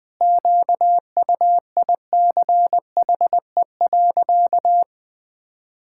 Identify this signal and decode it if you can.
Morse code, heard as a keyed tone, about 20 wpm, 700 Hz. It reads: QUICHE.
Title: Pangram Challenge v2 - 20wpm